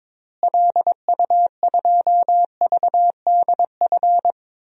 Morse code, heard as a keyed tone, about 22 wpm, 700 Hz. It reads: LU2VDF
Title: DXCC-100 Call Signs - 6 Characters - Repeated 3x - 22wpm